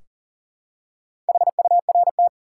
Morse code, heard as a keyed tone, about 40 wpm, 700 Hz. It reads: HURT
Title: Four Letter Words - 40wpm